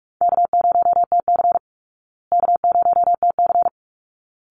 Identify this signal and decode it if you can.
Morse code, heard as a keyed tone, about 44 wpm, 700 Hz. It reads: K0TC K0TC
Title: Easy Call Signs Repeated 2x - 45wpm